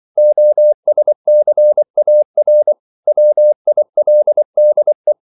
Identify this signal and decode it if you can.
Morse code, heard as a keyed tone, about 24 wpm, 600 Hz. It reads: OSCAR WILDE
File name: CW / Practice / EnglishQuotations0324